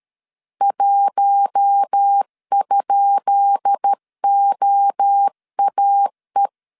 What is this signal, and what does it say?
Morse code, heard as a keyed tone, about 13 wpm, 800 Hz. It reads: 1?OAE
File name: LEZIONI / 10 SIGNS / 10B INT 12wpm 2414